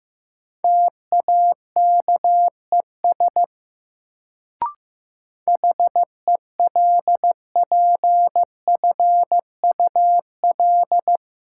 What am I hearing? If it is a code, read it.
Morse code, heard as a keyed tone, about 15 wpm, 700 Hz. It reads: TAKES HELPFUL
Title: Sentences from Top 1000 Words - Review Part 6 - New Word Drill - 15wpm